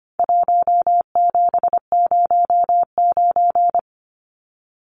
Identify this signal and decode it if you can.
Morse code, heard as a keyed tone, about 25 wpm, 700 Hz. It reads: 1709
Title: Numbers - Contest Style - 4-Digit 25wpm